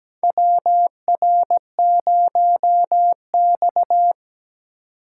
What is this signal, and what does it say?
Morse code, heard as a keyed tone, about 17 wpm, 700 Hz. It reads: WR0X
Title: USA Call Signs - 4x Character Spacing - 17wpm